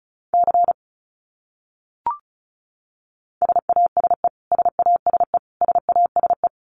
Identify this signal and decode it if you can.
Morse code, heard as a keyed tone, about 34 wpm, 700 Hz. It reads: C SASE SASE SASE
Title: Abbreviations Repeated 3x - 35wpm